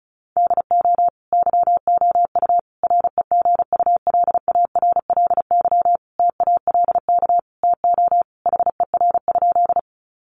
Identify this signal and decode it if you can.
Morse code, heard as a keyed tone, about 34 wpm, 700 Hz. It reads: DO YOU REGULARLY TALK TO HER?
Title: Sentences from Top 1400 Words 35wpm